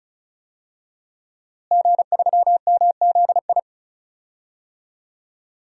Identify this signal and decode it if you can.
Morse code, heard as a keyed tone, about 36 wpm, 700 Hz. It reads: G3MZI
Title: DXCC-100 Call Signs - 5 Characters - Repeated 3x - 35wpm